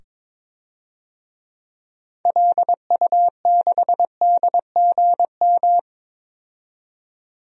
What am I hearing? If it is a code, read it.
Morse code, heard as a keyed tone, about 21 wpm, 700 Hz. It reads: LU6DGM
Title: DXCC-100 Call Signs - 6 Characters - 22wpm